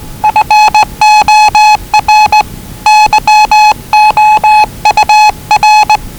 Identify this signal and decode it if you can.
Morse code, heard as a keyed tone, about 19 wpm, 840 Hz. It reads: FOR YOUR
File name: WellingtonRadioZlwCloseDownOnHfAndMf / 01-Zlw 4277khz 417.5 qrt until 0925z